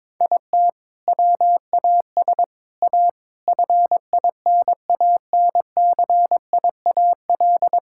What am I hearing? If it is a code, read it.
Morse code, heard as a keyed tone, about 22 wpm, 700 Hz. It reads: IT WAS A FINANCIAL
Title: Sentences from Top 1000 Words - Review Part 6 - 22wpm